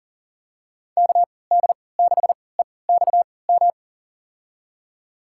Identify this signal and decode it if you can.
Morse code, heard as a keyed tone, about 40 wpm, spaced as for 23 wpm, 700 Hz. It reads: KD6EXM
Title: USA Call Signs - 6 Characters - 3x Character Spacing - 40wpm